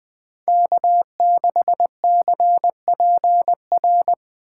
Morse code, heard as a keyed tone, about 20 wpm, 700 Hz. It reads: K6CPR
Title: 2023 SOTA Chaser Call Signs - 20wpm